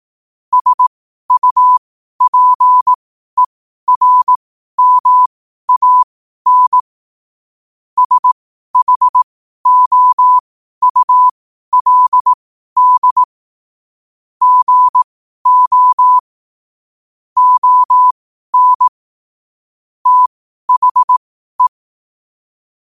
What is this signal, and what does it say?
Morse code, heard as a keyed tone, about 18 wpm, spaced as for 12 wpm, 1000 Hz. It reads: SUPERMAN SHOULD GO ON THE
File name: qotdfiles / 12wpm